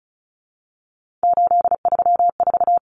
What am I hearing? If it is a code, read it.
Morse code, heard as a keyed tone, about 34 wpm, 700 Hz. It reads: 834
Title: Numbers - 3-Digit 35wpm